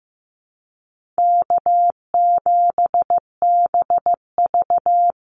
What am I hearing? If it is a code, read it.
Morse code, heard as a keyed tone, about 15 wpm, 700 Hz. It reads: K7BV
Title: Easy Call Signs Repeated 2x - 15wpm